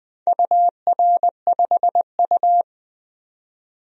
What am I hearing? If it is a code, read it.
Morse code, heard as a keyed tone, about 20 wpm, 700 Hz. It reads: UR5U
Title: DXCC-100 Call Signs - 4 Characters - 20wpm